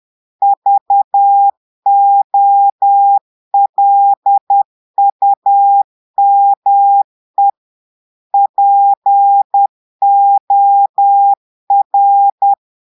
Morse code, heard as a keyed tone, about 10 wpm, 800 Hz. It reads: VOLUME POR
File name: Portugais - texte en clair 1 - vitesse - 10 mots minute